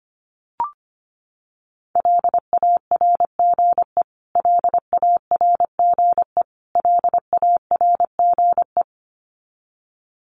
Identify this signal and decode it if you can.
Morse code, heard as a keyed tone, about 25 wpm, 700 Hz. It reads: LARGE LARGE LARGE
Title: Top 500 Words Repeated 25wpm